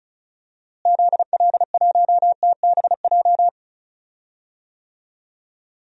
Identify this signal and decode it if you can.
Morse code, heard as a keyed tone, about 34 wpm, 700 Hz. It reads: ZL1TBJ
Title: DXCC-100 Call Signs - 6 Characters - Repeated 2x - 35wpm